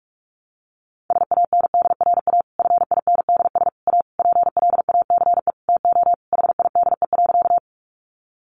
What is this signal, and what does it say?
Morse code, heard as a keyed tone, about 44 wpm, 700 Hz. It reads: SANDRA FINDS A PLACE TO HIDE.
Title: Easy Phrases 45wpm